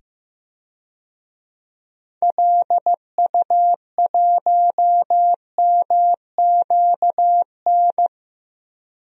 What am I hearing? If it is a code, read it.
Morse code, heard as a keyed tone, about 15 wpm, 700 Hz. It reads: LU1MQN
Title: DXCC-100 Call Signs - 6 Characters - 15wpm